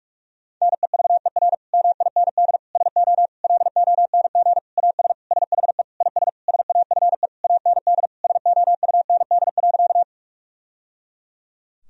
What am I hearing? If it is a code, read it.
Morse code, heard as a keyed tone, about 46 wpm, 700 Hz. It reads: NEVER MIND SO LONG AS SHE IS SAFE AND SOUND.
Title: Sentences from Top 1000 Words - Review Part 1 - 45wpm